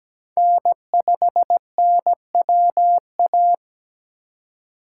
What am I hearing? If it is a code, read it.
Morse code, heard as a keyed tone, about 20 wpm, 700 Hz. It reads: N5NWA
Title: Speed Racing - USA Call Signs 17wpm